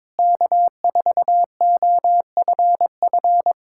Morse code, heard as a keyed tone, about 22 wpm, 700 Hz. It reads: K4OFF
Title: Speed-Racing Common QSO Phrases Encore - 22wpm